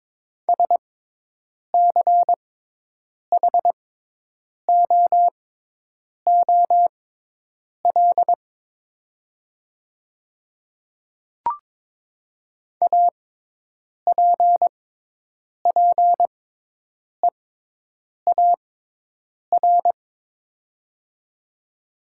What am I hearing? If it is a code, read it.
Morse code, heard as a keyed tone, about 22 wpm, spaced as for 8 wpm, 700 Hz. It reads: SCHOOL APPEAR
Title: 6x Character Spacing - Top 500 Words - 22wpm